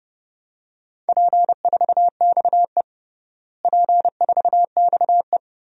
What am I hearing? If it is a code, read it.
Morse code, heard as a keyed tone, about 30 wpm, 700 Hz. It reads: P4XE P4XE
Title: DXCC-100 Call Signs - 4 Characters - Repeated 3x - 30wpm